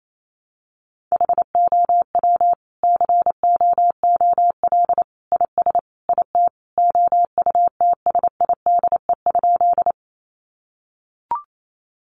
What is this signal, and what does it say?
Morse code, heard as a keyed tone, about 28 wpm, 700 Hz. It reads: HOW COOL IS IT OUTSIDE?
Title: Sentences from Top 1200 Words 28wpm